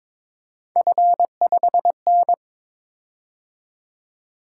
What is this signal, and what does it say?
Morse code, heard as a keyed tone, about 22 wpm, 700 Hz. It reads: F5N
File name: practice / 22wpm / Three Letter-Number Combinations - 22wpm